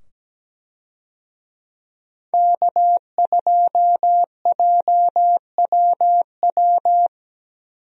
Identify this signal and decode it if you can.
Morse code, heard as a keyed tone, about 17 wpm, 700 Hz. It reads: K2JWW
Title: USA Call Signs - 5 Characters - 17wpm